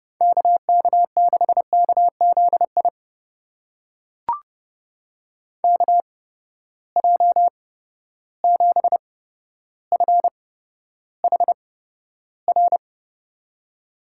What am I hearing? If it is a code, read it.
Morse code, heard as a keyed tone, about 30 wpm, spaced as for 13 wpm, 700 Hz. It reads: KK6KZI KJ7FHR
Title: USA Call Signs - 6 Characters - 8x Character Spacing - 30wpm